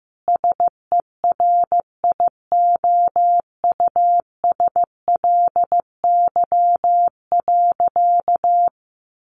Morse code, heard as a keyed tone, about 15 wpm, 700 Hz. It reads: SERIOUSLY.
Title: Sentences from Top 1200 Words 15wpm